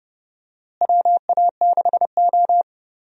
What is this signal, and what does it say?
Morse code, heard as a keyed tone, about 30 wpm, 700 Hz. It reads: WA6O
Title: USA Call Signs - 4 Characters - Repeated 2x - 30wpm